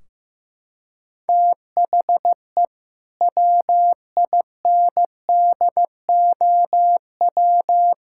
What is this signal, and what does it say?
Morse code, heard as a keyed tone, about 15 wpm, 700 Hz. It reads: THE WINDOW
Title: Sentences from Top 1600 Words 15wpm